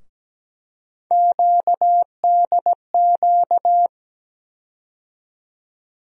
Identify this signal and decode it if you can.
Morse code, heard as a keyed tone, about 17 wpm, 700 Hz. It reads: QDQ
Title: USA Call Sign Postfixes - 3 Letters 17wpm